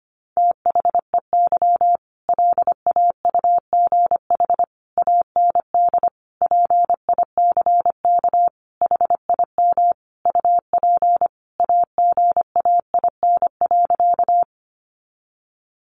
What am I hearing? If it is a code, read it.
Morse code, heard as a keyed tone, about 25 wpm, 700 Hz. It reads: THEY LAUGH AND PICK HIM UP AGAIN.
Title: Sentences from Top 1000 Words 25wpm